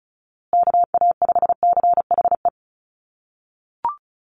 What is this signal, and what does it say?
Morse code, heard as a keyed tone, about 34 wpm, 700 Hz. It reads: KA5CHE
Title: USA Call Signs Repeated 3x 35wpm